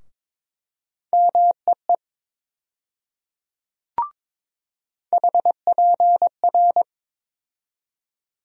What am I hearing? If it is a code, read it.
Morse code, heard as a keyed tone, about 22 wpm, 700 Hz. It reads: MEE HPR
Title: USA Call Sign Postfixes - 3 Letters 22wpm